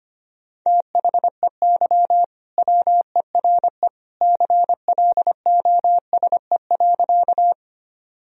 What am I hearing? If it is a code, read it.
Morse code, heard as a keyed tone, about 25 wpm, 700 Hz. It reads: THEY WERE CLOSE.
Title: Sentences from Top 1000 Words - Review Part 1 - 25wpm